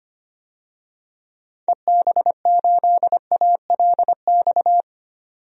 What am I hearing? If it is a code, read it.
Morse code, heard as a keyed tone, about 25 wpm, 700 Hz. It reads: EB8ALX
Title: DXCC-100 Call Signs - 6 Characters - 25wpm